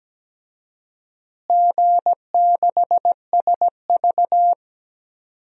Morse code, heard as a keyed tone, about 17 wpm, 700 Hz. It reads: G6SV
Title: DXCC-100 Call Signs - 4 Characters - Repeated 2x - 17wpm